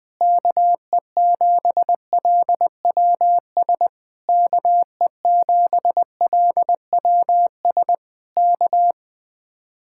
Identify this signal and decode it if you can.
Morse code, heard as a keyed tone, about 20 wpm, 700 Hz. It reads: KE7LWS KE7LWS K
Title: Speed-Racing Common QSO Phrases - 20wpm